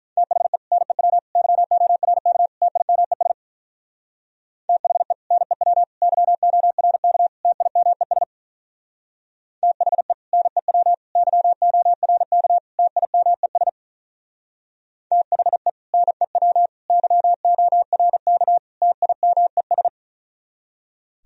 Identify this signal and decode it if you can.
Morse code, heard as a keyed tone, about 46 wpm, 700 Hz. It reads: THE NEW YORK TIMES THE NEW YORK TIMES THE NEW YORK TIMES THE NEW YORK TIMES